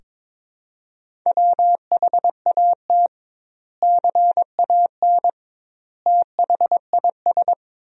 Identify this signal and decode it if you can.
Morse code, heard as a keyed tone, about 22 wpm, 700 Hz. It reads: WHAT CAN THIS
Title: 2x Word-Spacing - Sentences from Top 100 Words - 22wpm